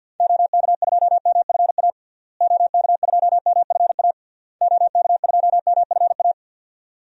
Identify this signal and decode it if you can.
Morse code, heard as a keyed tone, about 52 wpm, 700 Hz. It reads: OK2MFA OK2MFA OK2MFA
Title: DXCC-100 Call Signs - 6 Characters - Repeated 3x - 50wpm